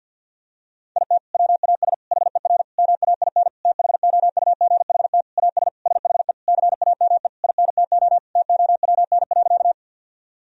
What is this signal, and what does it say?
Morse code, heard as a keyed tone, about 52 wpm, 700 Hz. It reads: IT WAS HER MAIN THOUGHT AS SHE CAME INTO TOWN.
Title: Sentences from Top 1000 Words - Review Part 2 - 50wpm